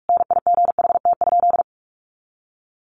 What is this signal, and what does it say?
Morse code, heard as a keyed tone, about 44 wpm, 700 Hz. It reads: NIGHT?
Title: Sentences from Top 1300 Words 45wpm